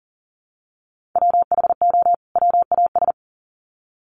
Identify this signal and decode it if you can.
Morse code, heard as a keyed tone, about 40 wpm, 700 Hz. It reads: WHO WAS
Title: Sets of 2 Words 40wpm